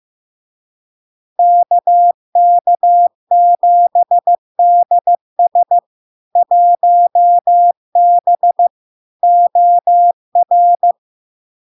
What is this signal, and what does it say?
Morse code, heard as a keyed tone, about 15 wpm, 700 Hz. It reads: KK7DS 1B OR
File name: LARK-FD-2022-Log-in-CW-15wpm-700-Hz